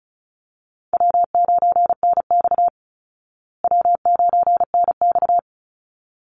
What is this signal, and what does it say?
Morse code, heard as a keyed tone, about 34 wpm, 700 Hz. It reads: W9NX W9NX
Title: Easy Call Signs Repeated 2x - 35wpm